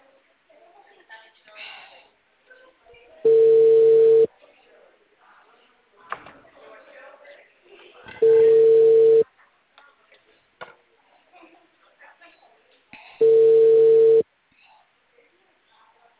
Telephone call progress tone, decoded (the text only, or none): ringback tone